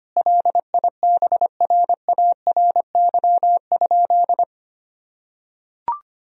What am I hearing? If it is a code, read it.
Morse code, heard as a keyed tone, about 25 wpm, 700 Hz. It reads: LIBRARY?
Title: Sentences from Top 1000 Words - Review Part 4 - 25wpm